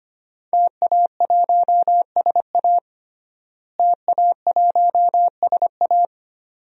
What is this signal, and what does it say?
Morse code, heard as a keyed tone, about 25 wpm, 700 Hz. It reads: TA1SA TA1SA
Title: DXCC-100 Call Signs - 5 Characters - Repeated 2x - 25wpm